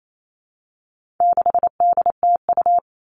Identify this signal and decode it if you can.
Morse code, heard as a keyed tone, about 28 wpm, 700 Hz. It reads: 6DTU
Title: All Cut Numbers - 4-Digit 28wpm